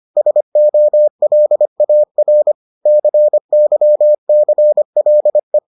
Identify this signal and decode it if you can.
Morse code, heard as a keyed tone, about 25 wpm, 600 Hz. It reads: SOLAR CYCLE